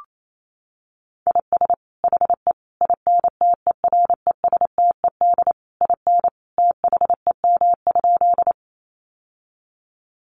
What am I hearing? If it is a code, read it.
Morse code, heard as a keyed tone, about 28 wpm, 700 Hz. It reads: IS HE INTERESTED IN THEM?
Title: Sentences from Top 900 Words 28wpm